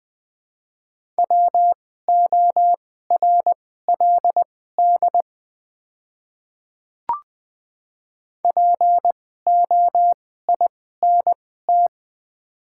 Morse code, heard as a keyed tone, about 20 wpm, spaced as for 14 wpm, 690 Hz. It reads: WORLD POINT
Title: 2x Character Spacing - Top 300 Words - 20wpm